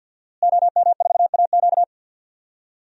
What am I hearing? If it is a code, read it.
Morse code, heard as a keyed tone, about 50 wpm, 700 Hz. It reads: OM4AQ
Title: DXCC-100 Call Signs - 5 Characters - Repeated 3x - 50wpm